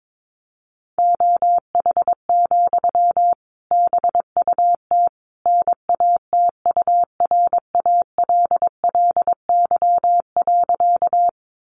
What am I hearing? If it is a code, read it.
Morse code, heard as a keyed tone, about 22 wpm, 700 Hz. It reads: OH, BUT NATURALLY.
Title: Sentences from Top 1300 Words 22wpm